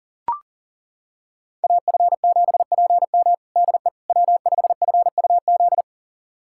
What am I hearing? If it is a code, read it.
Morse code, heard as a keyed tone, about 40 wpm, 700 Hz. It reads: AF7PM DE W5FUZ
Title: USA Callsign DE Callsign - 5 Characters - 40wpm